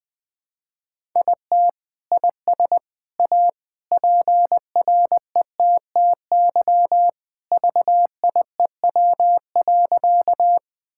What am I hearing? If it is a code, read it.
Morse code, heard as a keyed tone, about 20 wpm, 700 Hz. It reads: IT IS A PRETTY VIEW.
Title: Sentences from Top 1000 Words - Review Part 6 - 20wpm